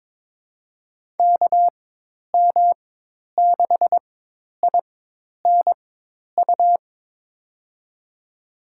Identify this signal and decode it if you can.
Morse code, heard as a keyed tone, about 21 wpm, spaced as for 10 wpm, 700 Hz. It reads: KM6INU
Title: USA Call Signs - 6 Characters - 4x Character Spacing - 22wpm